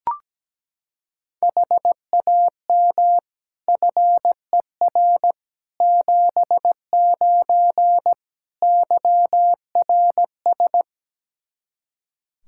Morse code, heard as a keyed tone, about 17 wpm, 700 Hz. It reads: HAM FER 79 YRS
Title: Practice QSO - Exchange 3 - More Info - 17wpm